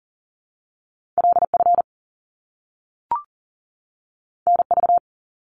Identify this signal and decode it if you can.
Morse code, heard as a keyed tone, about 39 wpm, 700 Hz. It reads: LF NV
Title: Top Two Letter Combinations in Words 40wpm